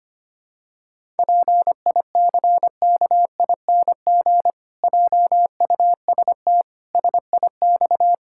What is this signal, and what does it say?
Morse code, heard as a keyed tone, about 25 wpm, 700 Hz. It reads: PICKING JUST SIX
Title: Pangram Challenge v2 - 25wpm